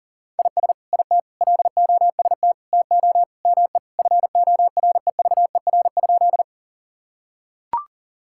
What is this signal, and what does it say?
Morse code, heard as a keyed tone, about 40 wpm, 700 Hz. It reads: IS IT LOST TO ME FOREVER?
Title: Sentences from Top 1600 Words 40wpm